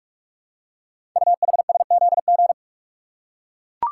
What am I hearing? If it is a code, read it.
Morse code, heard as a keyed tone, about 44 wpm, 700 Hz. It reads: UHSZG
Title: Code Groups - 5 Letters - 45wpm